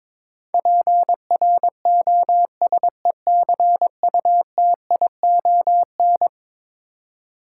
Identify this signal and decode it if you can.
Morse code, heard as a keyed tone, about 22 wpm, 700 Hz. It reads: PROSECUTION
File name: practice / 22wpm / Eleven Letter Words - 22wpm